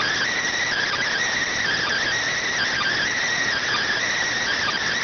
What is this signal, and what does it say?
Emergency Alert System (SAME) tones (header bursts), heard as harsh buzzing bursts. It is heard from 0.6 to 4.8 s.